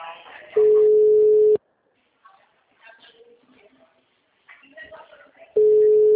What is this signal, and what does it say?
Ringback tone (Europe, 425 Hz), with speech.